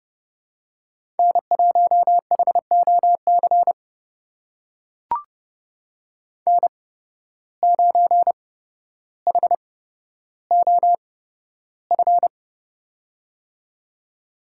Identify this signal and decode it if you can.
Morse code, heard as a keyed tone, about 30 wpm, spaced as for 13 wpm, 700 Hz. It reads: N1HOC N9HOF
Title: USA Call Signs - 5 Characters - 8x Character Spacing - 30wpm